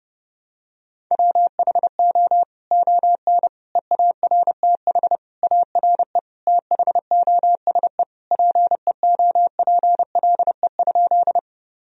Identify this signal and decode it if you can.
Morse code, heard as a keyed tone, about 30 wpm, 700 Hz. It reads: WHO ON EARTH ARE THOSE PEOPLE?
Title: Sentences from Top 1000 Words - Review Part 1 - 30wpm